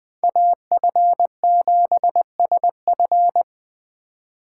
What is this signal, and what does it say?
Morse code, heard as a keyed tone, about 20 wpm, 700 Hz. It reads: AF7SF